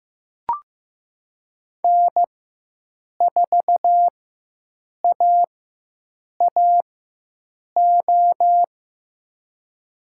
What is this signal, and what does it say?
Morse code, heard as a keyed tone, about 15 wpm, spaced as for 7 wpm, 700 Hz. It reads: N4AAO
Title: USA Call Signs - 5 Characters - 4x Character Spacing - 15wpm